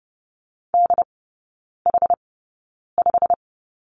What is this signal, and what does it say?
Morse code, heard as a keyed tone, about 30 wpm, spaced as for 9 wpm, 700 Hz. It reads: DH5
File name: Lesson 16e - Letter c - Three Characters 30wpm